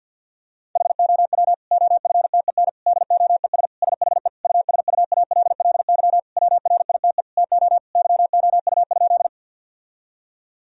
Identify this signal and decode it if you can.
Morse code, heard as a keyed tone, about 50 wpm, 700 Hz. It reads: HOW OFTEN DOES SHE USUALLY WRITE TO YOU?
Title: Sentences from Top 1000 Words - Review Part 3 - 50wpm